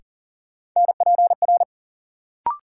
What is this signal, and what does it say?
Morse code, heard as a keyed tone, about 40 wpm, 700 Hz. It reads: NPR
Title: Sets of 5 Words - New Word Drill - 40wpm